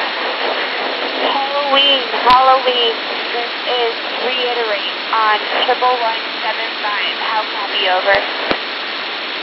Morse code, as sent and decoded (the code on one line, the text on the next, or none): none